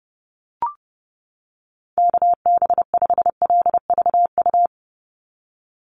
Morse code, heard as a keyed tone, about 30 wpm, 700 Hz. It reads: KB5LVU
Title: USA Call Signs 30wpm